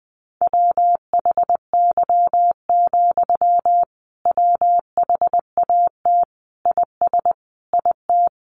Morse code, heard as a keyed tone, about 20 wpm, 700 Hz. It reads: WHY, WHAT IS IT
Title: Sentences from Top 200 Words 20wpm